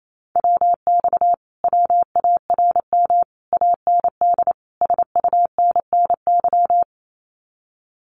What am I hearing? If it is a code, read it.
Morse code, heard as a keyed tone, about 28 wpm, 700 Hz. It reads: WX WARM AND SUNNY
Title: Common QSO Phrases Encore 28wpm